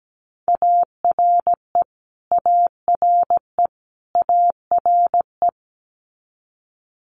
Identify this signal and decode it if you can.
Morse code, heard as a keyed tone, about 17 wpm, 700 Hz. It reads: ARE ARE ARE
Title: Multi-letter Phonemes Repeated 3x 17wpm